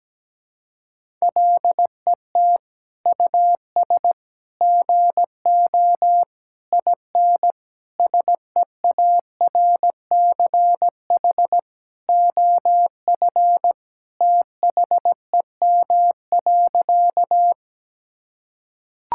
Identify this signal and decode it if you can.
Morse code, heard as a keyed tone, about 17 wpm, 700 Hz. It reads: LET US GO IN SEARCH OF THEM.
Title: Sentences from Top 1000 Words - Review Part 5 - 17wpm